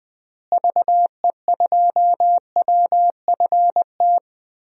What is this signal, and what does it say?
Morse code, heard as a keyed tone, about 20 wpm, 700 Hz. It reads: VE2WFT
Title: DXCC-100 Call Signs - 6 Characters - Repeated 3x - 20wpm